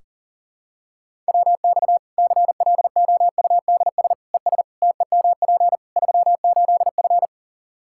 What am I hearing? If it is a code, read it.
Morse code, heard as a keyed tone, about 40 wpm, 700 Hz. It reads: WX CLOUDS ES TEMP 38F
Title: Practice QSO - Exchange 2 - Info - 40wpm